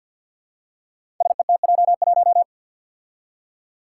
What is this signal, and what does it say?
Morse code, heard as a keyed tone, about 53 wpm, 700 Hz. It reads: SETJ1
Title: Five Challenging Letter-Number Combinations 50wpm